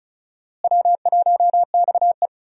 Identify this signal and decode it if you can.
Morse code, heard as a keyed tone, about 36 wpm, 700 Hz. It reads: W1XE
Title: USA Call Signs - 4 Characters - Repeated 3x - 35wpm